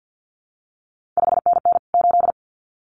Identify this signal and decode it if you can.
Morse code, heard as a keyed tone, about 50 wpm, 700 Hz. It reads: 5NN 8
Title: CQ WW DX Contest Exchange - 50wpm